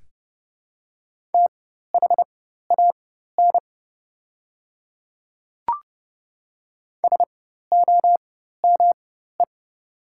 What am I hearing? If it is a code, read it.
Morse code, heard as a keyed tone, about 30 wpm, spaced as for 14 wpm, 690 Hz. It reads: THAN SOME
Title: 4x Character Spacing - Top 200 Words - 30wpm